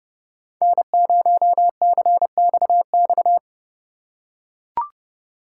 Morse code, heard as a keyed tone, about 29 wpm, 700 Hz. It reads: N0CXX